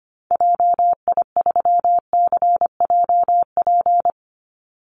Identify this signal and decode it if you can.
Morse code, heard as a keyed tone, about 25 wpm, 700 Hz. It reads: JI3CJP
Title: Call Signs Repeated 3x - 25wpm